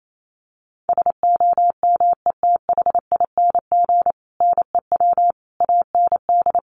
Morse code, heard as a keyed tone, about 28 wpm, 700 Hz. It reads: SOMETHING NEW AND
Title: Sentences from Top 700 Words 28wpm